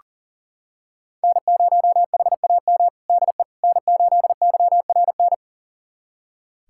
Morse code, heard as a keyed tone, about 40 wpm, 700 Hz. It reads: N0HAM DE N8YRN
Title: USA Callsign DE Callsign - 5 Characters - 40wpm